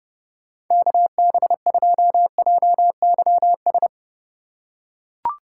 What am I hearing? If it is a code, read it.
Morse code, heard as a keyed tone, about 30 wpm, 700 Hz. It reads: KB2JYS
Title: USA Call Signs - 6 Characters - 2x Character Spacing - 30wpm